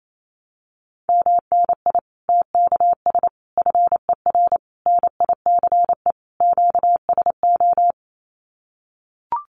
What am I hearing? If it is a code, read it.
Morse code, heard as a keyed tone, about 28 wpm, 700 Hz. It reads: MNI TKS FER NICE QSO
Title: Common QSO Phrases 28wpm